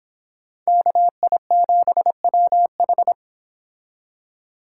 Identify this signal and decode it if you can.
Morse code, heard as a keyed tone, about 30 wpm, 700 Hz. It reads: KI7WH